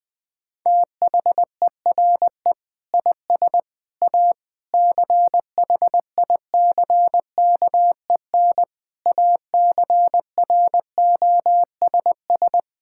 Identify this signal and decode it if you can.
Morse code, heard as a keyed tone, about 20 wpm, 700 Hz. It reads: THERE IS A CHICKEN ACROSS
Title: Sentences from Top 1000 Words - Review Part 5 - 20wpm